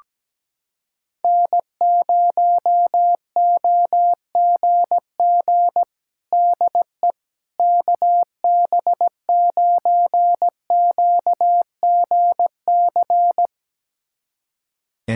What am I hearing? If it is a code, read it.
Morse code, heard as a keyed tone, about 17 wpm, 700 Hz. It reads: N0OGG DE KB9QGC
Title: USA Callsign DE Callsign 17wpm